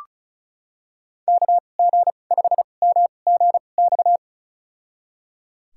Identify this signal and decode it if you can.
Morse code, heard as a keyed tone, about 34 wpm, spaced as for 25 wpm, 700 Hz. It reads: KG5MGX
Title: USA Call Signs - 6 Characters - 2x Character Spacing - 35wpm